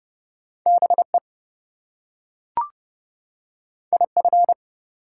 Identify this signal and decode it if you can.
Morse code, heard as a keyed tone, about 30 wpm, 700 Hz. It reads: BE IF
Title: Two Letter Words - 30wpm